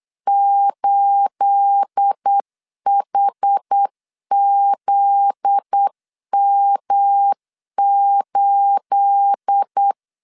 Morse code, heard as a keyed tone, about 8 wpm, 800 Hz. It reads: 8HZM8
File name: LEZIONI / 9 28 8wpm 3512